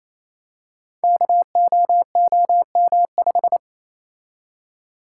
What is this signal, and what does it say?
Morse code, heard as a keyed tone, about 28 wpm, 700 Hz. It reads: KOOM5
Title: Code Groups - 5 Letter:Numbers - 28wpm